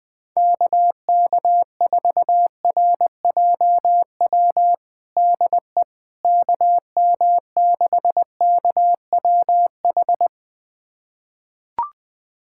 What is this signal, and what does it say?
Morse code, heard as a keyed tone, about 21 wpm, 700 Hz. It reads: KK4RJW DE KM6KWH
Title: USA Callsign DE Callsign 20wpm